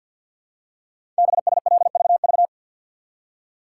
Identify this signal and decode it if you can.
Morse code, heard as a keyed tone, about 51 wpm, 700 Hz. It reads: BSLVV
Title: Code Groups - 5 Letters - 50wpm